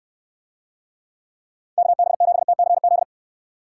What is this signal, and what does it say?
Morse code, heard as a keyed tone, about 69 wpm, 700 Hz. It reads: KB7TCZ